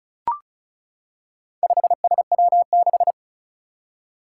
Morse code, heard as a keyed tone, about 36 wpm, 700 Hz. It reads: 5SW6
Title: Four Challenging Letter-Number Combinations 35wpm